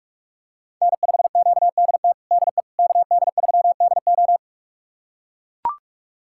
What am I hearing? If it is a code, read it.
Morse code, heard as a keyed tone, about 41 wpm, 700 Hz. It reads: N5QDT DE KD3DO
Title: USA Callsign DE Callsign - 5 Characters - 45wpm